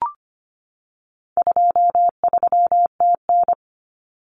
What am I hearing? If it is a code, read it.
Morse code, heard as a keyed tone, about 25 wpm, 700 Hz. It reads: 23TN
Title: Common Cut Numbers - 4-Digit 25wpm